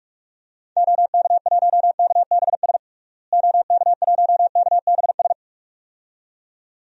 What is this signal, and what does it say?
Morse code, heard as a keyed tone, about 46 wpm, 700 Hz. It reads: OK1KBS OK1KBS
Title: DXCC-100 Call Signs - 6 Characters - Repeated 2x - 45wpm